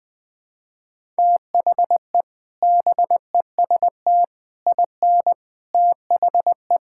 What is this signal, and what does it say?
Morse code, heard as a keyed tone, about 21 wpm, 700 Hz. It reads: THE BEST IN THE
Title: Sentences with 3-Word Phrases - 20wpm